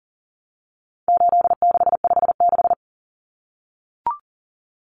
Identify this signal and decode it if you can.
Morse code, heard as a keyed tone, about 40 wpm, 700 Hz. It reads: 8656
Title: Common Cut Numbers - 4-Digit 40wpm